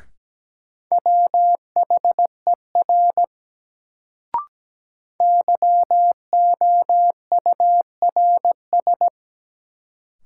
Morse code, heard as a keyed tone, about 17 wpm, 700 Hz. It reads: WHER YOURS
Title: Sentences from Top 1500 Words - New Word Drill - 17wpm